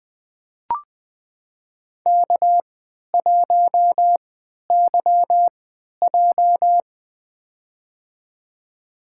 Morse code, heard as a keyed tone, about 20 wpm, spaced as for 11 wpm, 700 Hz. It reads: K1YJ
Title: USA Call Signs - 4 Characters - 3x Character Spacing - 20wpm